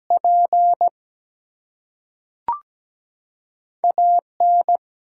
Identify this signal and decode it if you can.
Morse code, heard as a keyed tone, about 17 wpm, 700 Hz. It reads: P AN
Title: Two Letter Words - 17wpm